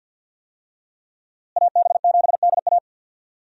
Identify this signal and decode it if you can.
Morse code, heard as a keyed tone, about 52 wpm, 700 Hz. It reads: AB7DA
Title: USA Call Signs - 5 Characters - Repeated 3x - 50wpm